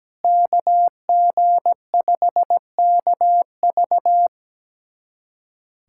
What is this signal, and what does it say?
Morse code, heard as a keyed tone, about 17 wpm, 700 Hz. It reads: KG5KV
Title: USA Callsign DE Callsign - 5 Characters - 17wpm